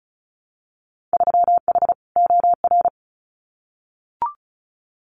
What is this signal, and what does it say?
Morse code, heard as a keyed tone, about 36 wpm, 700 Hz. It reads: 3H OR
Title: Winter Field Day Exchanges Encore 35wpm